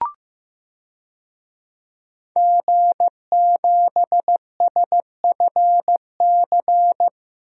Morse code, heard as a keyed tone, about 15 wpm, 700 Hz. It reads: G7SFC
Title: DXCC-100 Call Signs - 5 Characters - 15wpm